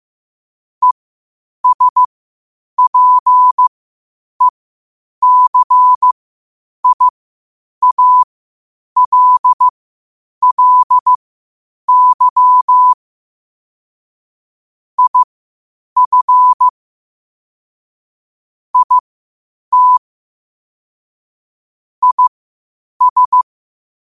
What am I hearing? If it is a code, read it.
Morse code, heard as a keyed tone, about 15 wpm, spaced as for 8 wpm, 1000 Hz. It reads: ESPECIALLY IF IT IS
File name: qotdfiles / 8wpm